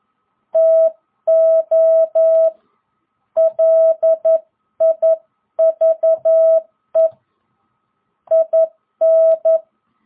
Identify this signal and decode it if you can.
Morse code, heard as a keyed tone, about 11 wpm, 660 Hz. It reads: TO LIVE IN